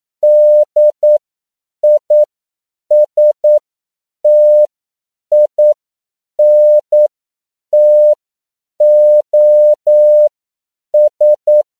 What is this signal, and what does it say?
Morse code, heard as a keyed tone, about 9 wpm, spaced as for 7 wpm, 600 Hz. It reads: DISTINTOS
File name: CW / 10PPM